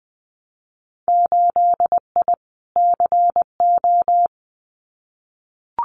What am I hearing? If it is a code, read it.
Morse code, heard as a keyed tone, about 20 wpm, 700 Hz. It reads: 8I CO
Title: Winter Field Day Exchanges Encore 20wpm